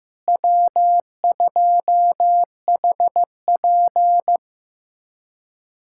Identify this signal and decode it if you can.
Morse code, heard as a keyed tone, about 15 wpm, 700 Hz. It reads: W2HP
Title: USA Callsign DE Callsign - 4 Characters - 15wpm